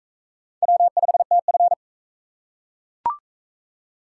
Morse code, heard as a keyed tone, about 38 wpm, 700 Hz. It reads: W5TF